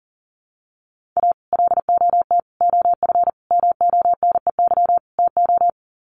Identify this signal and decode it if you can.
Morse code, heard as a keyed tone, about 40 wpm, 700 Hz. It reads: A LOT OF MONEY TO
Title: Sets of 5 Words Encore2 40wpm